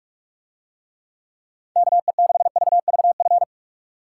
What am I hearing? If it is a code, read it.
Morse code, heard as a keyed tone, about 43 wpm, 700 Hz. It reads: KE6VVF